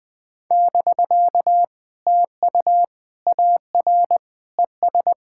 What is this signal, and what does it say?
Morse code, heard as a keyed tone, about 20 wpm, 700 Hz. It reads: <BK> TU AR ES
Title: POTA QSO Practice - 20wpm